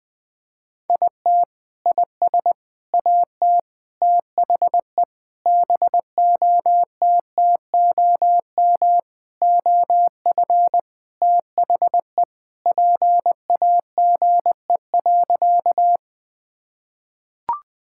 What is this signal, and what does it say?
Morse code, heard as a keyed tone, about 20 wpm, 700 Hz. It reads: IT IS AT THE BOTTOM OF THE PAGE.
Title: Sentences with 4-Word Phrases - 20wpm